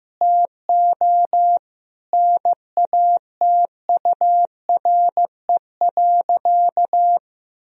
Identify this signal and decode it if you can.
Morse code, heard as a keyed tone, about 15 wpm, 700 Hz. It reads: TO NATURE.
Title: Sentences from Top 1000 Words - Review Part 4 - 15wpm